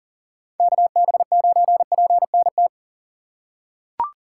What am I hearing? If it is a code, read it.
Morse code, heard as a keyed tone, about 40 wpm, 700 Hz. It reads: KB9PNT
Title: USA Call Signs - 6 Characters - 6x Character Spacing - 40wpm